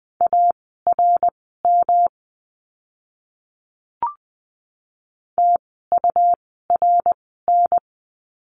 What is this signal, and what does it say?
Morse code, heard as a keyed tone, about 20 wpm, spaced as for 15 wpm, 700 Hz. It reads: ARM TURN
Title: 108 Lesson 18f - Letter m - Words 20wpm